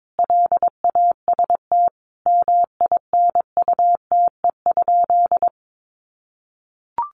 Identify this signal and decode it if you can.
Morse code, heard as a keyed tone, about 22 wpm, 700 Hz. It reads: LAST MINUTE?
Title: Sentences from Top 1100 Words 22wpm